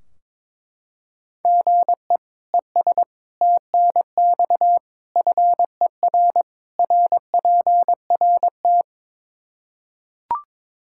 Morse code, heard as a keyed tone, about 22 wpm, 700 Hz. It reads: GE ES TNX FER RPRT
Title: Practice QSO - Exchange 1 - Beginning - 22wpm